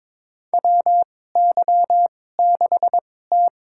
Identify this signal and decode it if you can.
Morse code, heard as a keyed tone, about 22 wpm, spaced as for 16 wpm, 700 Hz. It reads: WY6T
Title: USA Call Signs - 4 Characters - 2x Character Spacing - 22wpm